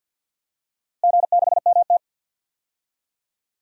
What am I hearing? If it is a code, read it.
Morse code, heard as a keyed tone, about 51 wpm, 700 Hz. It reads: G6MT
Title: DXCC-100 Call Signs - 4 Characters - Repeated 2x - 50wpm